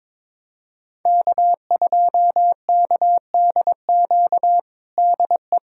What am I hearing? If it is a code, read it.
Morse code, heard as a keyed tone, about 22 wpm, 700 Hz. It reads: K2KDQ DE
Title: USA Callsign DE Callsign - 5 Characters - 22wpm